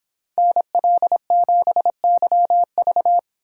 Morse code, heard as a keyed tone, about 25 wpm, 700 Hz. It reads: NL7YV